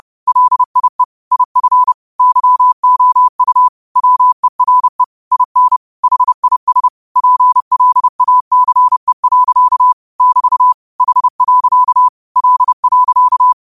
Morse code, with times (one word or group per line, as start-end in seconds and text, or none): 0.3-1.0 LIE
1.3-1.9 IF
2.2-3.7 YOU
4.0-5.0 WERE
5.3-5.8 IN
6.0-6.9 HIS
7.2-9.9 PLACE.
10.2-10.7 <BT>
11.0-12.1 H.
12.4-13.5 L.